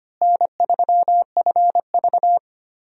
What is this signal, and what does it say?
Morse code, heard as a keyed tone, about 25 wpm, 700 Hz. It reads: N3FV
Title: USA Call Signs - 4 Characters - Repeated 3x - 25wpm